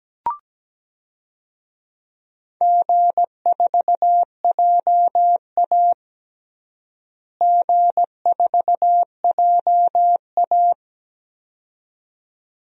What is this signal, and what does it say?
Morse code, heard as a keyed tone, about 17 wpm, 700 Hz. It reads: G4JA G4JA
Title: DXCC-100 Call Signs - 4 Characters - Repeated 2x - 17wpm